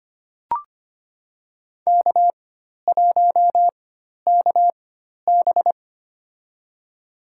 Morse code, heard as a keyed tone, about 25 wpm, spaced as for 12 wpm, 700 Hz. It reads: K1KB
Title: USA Call Signs - 4 Characters - 4x Character Spacing - 25wpm